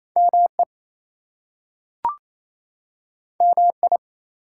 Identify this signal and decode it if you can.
Morse code, heard as a keyed tone, about 28 wpm, 700 Hz. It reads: ME MI
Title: US States and Canadian Provinces 28wpm